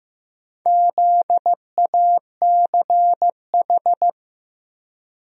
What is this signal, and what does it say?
Morse code, heard as a keyed tone, about 15 wpm, 700 Hz. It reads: ZACH
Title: USA Common Names - Max 4 Letters - 15wpm